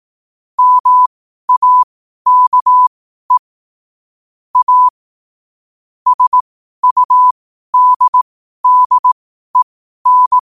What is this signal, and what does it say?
Morse code, heard as a keyed tone, about 18 wpm, spaced as for 12 wpm, 1000 Hz. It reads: MAKE A SUDDEN
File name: qotdfiles / 12wpm